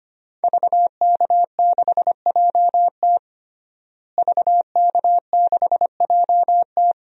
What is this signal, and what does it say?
Morse code, heard as a keyed tone, about 25 wpm, 700 Hz. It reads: VK6JT VK6JT
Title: DXCC-100 Call Signs - 5 Characters - Repeated 2x - 25wpm